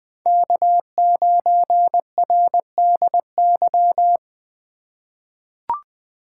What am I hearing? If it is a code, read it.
Morse code, heard as a keyed tone, about 21 wpm, 700 Hz. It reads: K9RDY
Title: USA Call Signs - 5 Characters - 2x Character Spacing - 20wpm